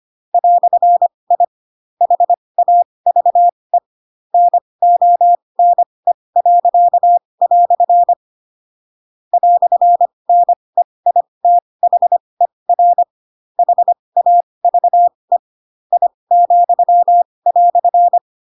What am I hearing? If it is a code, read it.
Morse code, heard as a keyed tone, about 25 wpm, spaced as for 20 wpm, 700 Hz. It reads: "I HAVE NONE." "NEITHER HAVE I,"
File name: Pinocchio / Pinocchio ch8 15 25 ik1hge